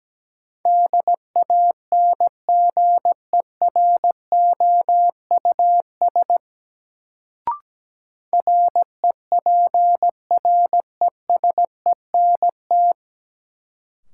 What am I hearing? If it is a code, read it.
Morse code, heard as a keyed tone, about 17 wpm, 700 Hz. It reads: DANGEROUS REPRESENT
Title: Sentences from Top 1200 Words - New Word Drill - 17wpm